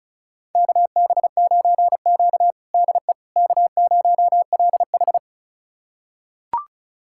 Morse code, heard as a keyed tone, about 36 wpm, 700 Hz. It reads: KB9Q DE K0LH
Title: USA Callsign DE Callsign - 4 Characters - 35wpm